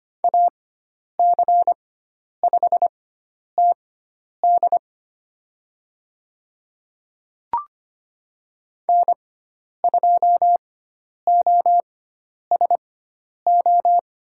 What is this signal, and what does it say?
Morse code, heard as a keyed tone, about 25 wpm, spaced as for 10 wpm, 700 Hz. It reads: AC5TD N2OSO
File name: Lesson 16g - Letter c - Call Signs 25wpm